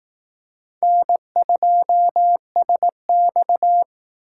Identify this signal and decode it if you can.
Morse code, heard as a keyed tone, about 18 wpm, 700 Hz. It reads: N2SX
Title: Speed Racing - USA 1x2 Call Signs 15wpm